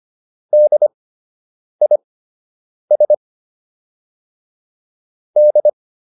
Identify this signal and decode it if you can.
Morse code, heard as a keyed tone, about 25 wpm, spaced as for 8 wpm, 600 Hz. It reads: DIS D